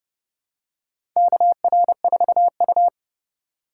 Morse code, heard as a keyed tone, about 30 wpm, 700 Hz. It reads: KR4U
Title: USA Call Signs - 4 Characters - 2x Character Spacing - 30wpm